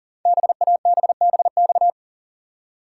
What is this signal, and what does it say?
Morse code, heard as a keyed tone, about 44 wpm, 700 Hz. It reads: BABBX
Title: Code Groups - 5 Letters - 40wpm